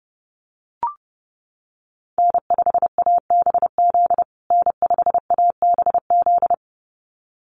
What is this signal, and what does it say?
Morse code, heard as a keyed tone, about 30 wpm, 700 Hz. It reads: N5ABZ N5ABZ
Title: USA Call Signs Repeated 2x 30wpm